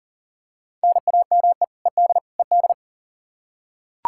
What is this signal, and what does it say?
Morse code, heard as a keyed tone, about 40 wpm, 700 Hz. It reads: NAME ED ED
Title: Practice QSO - Exchange 1 - Beginning - 40wpm